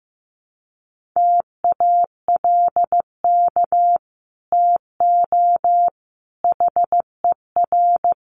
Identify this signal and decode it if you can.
Morse code, heard as a keyed tone, about 15 wpm, 700 Hz. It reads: TALK TO HER
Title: Sentences from Top 1500 Words 15wpm